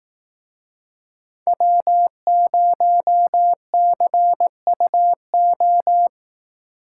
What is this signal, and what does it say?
Morse code, heard as a keyed tone, about 18 wpm, 700 Hz. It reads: W0CUO
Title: Speed Racing - USA 1x3 Call Signs 15wpm